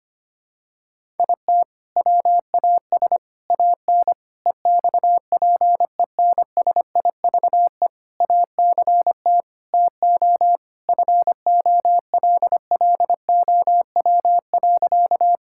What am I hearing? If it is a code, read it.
Morse code, heard as a keyed tone, about 25 wpm, 700 Hz. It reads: IT WAS AN EXPENSIVE ACT TO FOLLOW.
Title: Sentences from Top 1000 Words - Review Part 6 - 25wpm